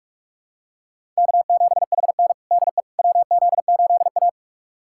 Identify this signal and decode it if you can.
Morse code, heard as a keyed tone, about 44 wpm, 700 Hz. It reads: K7HN DE WZ8A
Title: USA Callsign DE Callsign - 4 Characters - 45wpm